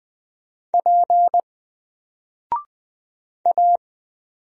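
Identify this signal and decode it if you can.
Morse code, heard as a keyed tone, about 20 wpm, 700 Hz. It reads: P A
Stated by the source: Single letter and number Morse drill at 20 wpm, from N4LCD.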